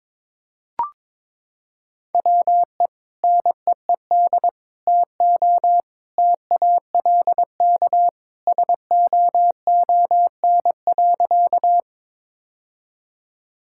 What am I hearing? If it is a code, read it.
Morse code, heard as a keyed tone, about 22 wpm, 700 Hz. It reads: WE NEED TO TALK SOON.
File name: practice / 22wpm / Sentences with 2-Word Phrases - 22wpm